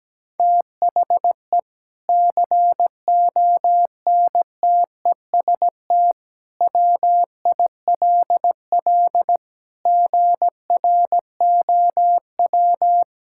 Morse code, heard as a keyed tone, about 17 wpm, 700 Hz. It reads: THE CONTEST WILL GROW
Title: Sentences from Top 1600 Words 17wpm